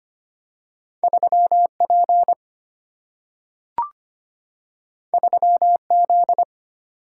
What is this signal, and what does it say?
Morse code, heard as a keyed tone, about 25 wpm, 700 Hz. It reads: 3P 3Z
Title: Single-Letter Single-Number 25wpm